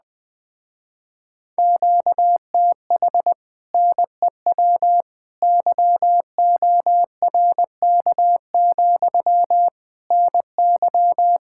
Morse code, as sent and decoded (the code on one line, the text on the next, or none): --.- - .... / -. .-- / -.-- --- .-. -.- --..-- / -. -.--
QTH NEW YORK, NY